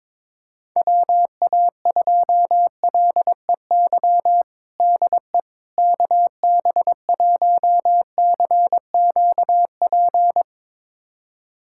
Morse code, heard as a keyed tone, about 21 wpm, 700 Hz. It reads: WA2LEY DE KB1CQP
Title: USA Callsign DE Callsign 22wpm